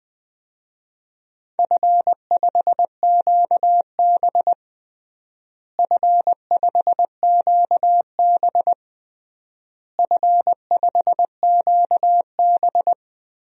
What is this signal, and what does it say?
Morse code, heard as a keyed tone, about 20 wpm, 700 Hz. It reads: F5QB F5QB F5QB
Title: DXCC-100 Call Signs - 4 Characters - Repeated 3x - 20wpm